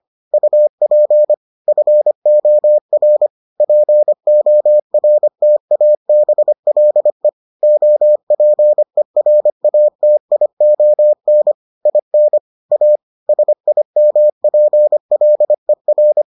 Morse code, heard as a keyed tone, about 25 wpm, 600 Hz. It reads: UP FOR PORTABLE OPERATION IN A SIMPLER